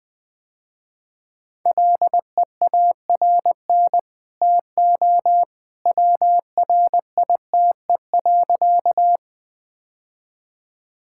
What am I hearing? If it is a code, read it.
Morse code, heard as a keyed tone, about 21 wpm, 700 Hz. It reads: LEARN TO WRITE.
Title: Easy Sentences with Hints - 20wpm